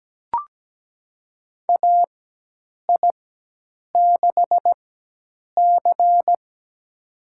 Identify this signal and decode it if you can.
Morse code, heard as a keyed tone, about 17 wpm, spaced as for 8 wpm, 700 Hz. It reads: AI6C
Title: USA Call Signs - 4 Characters - 4x Character Spacing - 17wpm